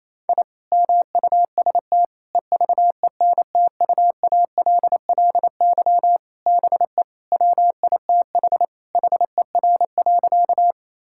Morse code, heard as a keyed tone, about 28 wpm, 700 Hz. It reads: I MUST EVENTUALLY BE WITH HER.
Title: Sentences from Top 1000 Words - Review Part 6 - 28wpm